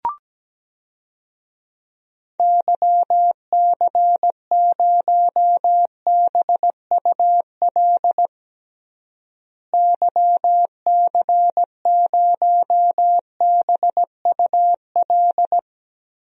Morse code, heard as a keyed tone, about 17 wpm, 700 Hz. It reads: YC0BUL YC0BUL
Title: DXCC-100 Call Signs - 6 Characters - Repeated 3x - 17wpm